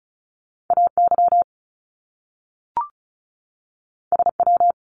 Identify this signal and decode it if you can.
Morse code, heard as a keyed tone, about 36 wpm, 700 Hz. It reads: AY SW
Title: Top Two Letter Combinations in Words 35wpm